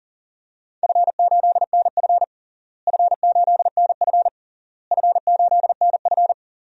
Morse code, heard as a keyed tone, about 40 wpm, 700 Hz. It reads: F8NF F8NF F8NF
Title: DXCC-100 Call Signs - 4 Characters - Repeated 3x - 40wpm